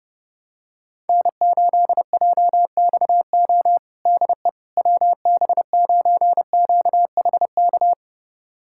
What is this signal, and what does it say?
Morse code, heard as a keyed tone, about 30 wpm, 700 Hz. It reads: N8JXO DE WB9QHK
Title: USA Callsign DE Callsign 30wpm